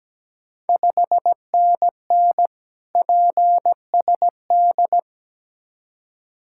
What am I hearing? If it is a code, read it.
Morse code, heard as a keyed tone, about 17 wpm, 690 Hz. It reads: 5NN PSD
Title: Ontario QSO Party Exchange 17wpm